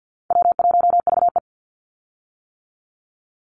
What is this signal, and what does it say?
Morse code, heard as a keyed tone, about 48 wpm, 700 Hz. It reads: W1VE
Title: Easy Call Signs - 50wpm